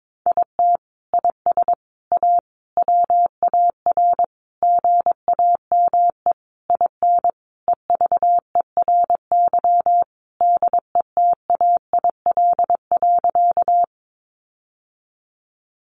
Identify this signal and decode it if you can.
Morse code, heard as a keyed tone, about 22 wpm, 700 Hz. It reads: IT IS A WAR GAME IN EVERY DETAIL.
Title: Sentences from Top 1500 Words 22wpm